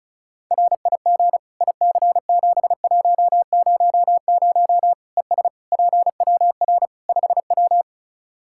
Morse code, heard as a keyed tone, about 36 wpm, 700 Hz. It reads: RIG IC7100 ES PWR 5W
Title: Practice QSO - Exchange 2 - Info - 35wpm